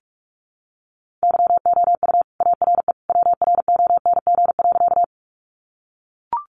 Morse code, heard as a keyed tone, about 44 wpm, 700 Hz. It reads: YOU ARE WRONG.
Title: Sentences from Top 900 Words 45wpm